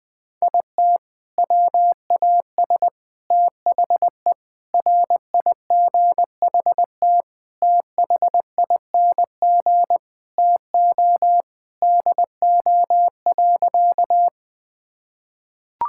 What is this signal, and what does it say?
Morse code, heard as a keyed tone, about 20 wpm, 700 Hz. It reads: IT WAS THE RIGHT THING TO DO.
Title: Sentences with 6-Word Phrases - 20wpm